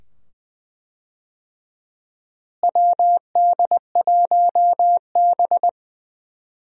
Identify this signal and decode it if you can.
Morse code, heard as a keyed tone, about 20 wpm, 700 Hz. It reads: WD1B
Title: USA Call Signs - 4 Characters - Repeated 2x - 20wpm